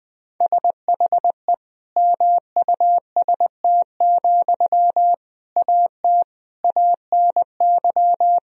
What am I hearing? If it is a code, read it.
Morse code, heard as a keyed tone, about 20 wpm, 700 Hz. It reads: SHE MUST, AT ANY
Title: Sentences from Top 1000 Words - Review Part 4 - 20wpm